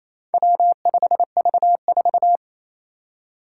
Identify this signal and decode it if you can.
Morse code, heard as a keyed tone, about 29 wpm, 700 Hz. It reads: W5V4
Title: Four Challenging Letter-Number Combinations 28wpm